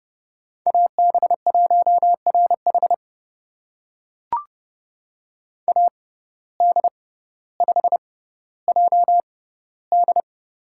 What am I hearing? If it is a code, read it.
Morse code, heard as a keyed tone, about 30 wpm, spaced as for 15 wpm, 700 Hz. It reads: AB1RH AD5JD
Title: USA Call Signs - 5 Characters - 6x Character Spacing - 30wpm